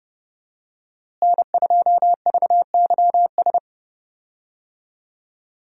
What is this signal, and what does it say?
Morse code, heard as a keyed tone, about 30 wpm, 700 Hz. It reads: N2VYS
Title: USA Call Signs - 5 Characters - Repeated 2x - 30wpm